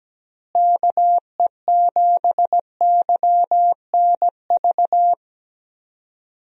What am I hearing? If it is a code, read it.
Morse code, heard as a keyed tone, about 17 wpm, 700 Hz. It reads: KE7YNV
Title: USA Call Signs - 6 Characters - Repeated 3x - 17wpm